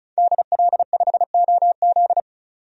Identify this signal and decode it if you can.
Morse code, heard as a keyed tone, about 36 wpm, 700 Hz. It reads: DL5OZ
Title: DXCC-100 Call Signs - 5 Characters - 35wpm